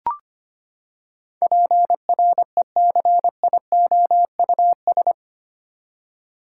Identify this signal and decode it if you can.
Morse code, heard as a keyed tone, about 25 wpm, 700 Hz. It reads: PRECIOUS
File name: Eight Letter Words - 25wpm